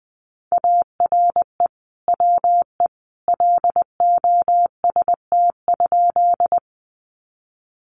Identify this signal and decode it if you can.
Morse code, heard as a keyed tone, about 20 wpm, 700 Hz. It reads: ARE WE LOST?
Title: Sentences from Top 1400 Words 20wpm